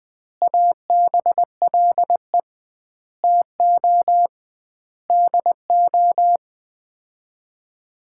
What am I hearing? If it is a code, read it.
Morse code, heard as a keyed tone, about 20 wpm, 700 Hz. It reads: ABLE TO DO
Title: 2x Word-Spacing - Sets of 5 Words - 20wpm